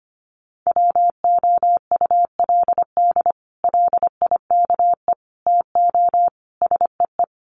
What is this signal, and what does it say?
Morse code, heard as a keyed tone, about 25 wpm, 700 Hz. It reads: WOULD LIKE TO SEE
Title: Sets of 4 Words 25wpm